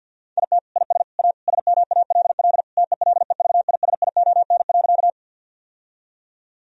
Morse code, heard as a keyed tone, about 50 wpm, 700 Hz. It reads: IT IS A SMALL TELEVISION.
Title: Sentences from Top 1000 Words - Review Part 4 - 50wpm